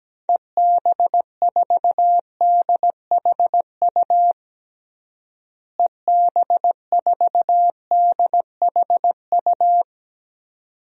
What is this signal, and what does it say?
Morse code, heard as a keyed tone, about 17 wpm, 700 Hz. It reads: EB4DHU EB4DHU
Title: DXCC-100 Call Signs - 6 Characters - Repeated 2x - 17wpm